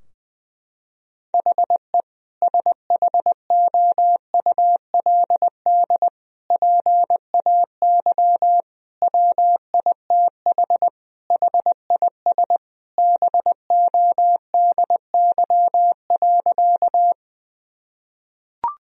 Morse code, heard as a keyed tone, about 20 wpm, 700 Hz. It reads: HE SHOULD PAY WITH HIS BODY.
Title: Sentences from Top 1000 Words - Review Part 2 - 20wpm